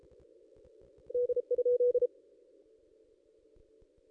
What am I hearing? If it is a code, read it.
Morse code, heard as a keyed tone, about 34 wpm, 490 Hz. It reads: D?
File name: contests / ss-cw-k5tr-op-n5rz -1900